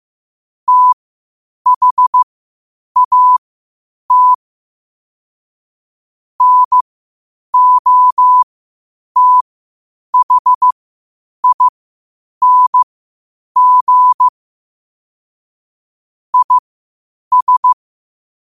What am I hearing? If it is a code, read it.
Morse code, heard as a keyed tone, about 15 wpm, spaced as for 8 wpm, 1000 Hz. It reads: THAT NOTHING IS
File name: qotdfiles / 8wpm